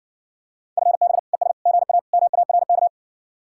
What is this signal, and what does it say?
Morse code, heard as a keyed tone, about 60 wpm, 700 Hz. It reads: FB ES GA MARK